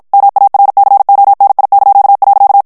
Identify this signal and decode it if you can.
Morse code, heard as a keyed tone, about 50 wpm, 800 Hz. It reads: MARCONI).
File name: morse mp3 / ITALIEN texte clair 1 vitesse 50 mots minute